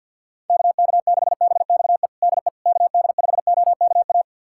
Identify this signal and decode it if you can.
Morse code, heard as a keyed tone, about 51 wpm, 700 Hz. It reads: KK6BXE DE KD5OKA
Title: USA Callsign DE Callsign - 6 Characters - 50wpm